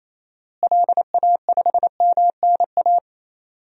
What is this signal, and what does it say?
Morse code, heard as a keyed tone, about 28 wpm, 700 Hz. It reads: LA5MNA
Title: DXCC-100 Call Signs - 6 Characters - Repeated 2x - 28wpm